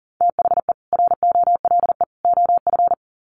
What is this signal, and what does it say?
Morse code, heard as a keyed tone, about 40 wpm, 700 Hz. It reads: THE ROLE OF
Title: Sets of 3 Words 40wpm